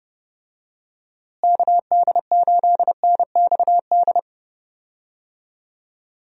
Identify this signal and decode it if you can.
Morse code, heard as a keyed tone, about 32 wpm, 700 Hz. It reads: KD8NXD
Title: USA Call Signs - 6 Characters - Repeated 3x - 30wpm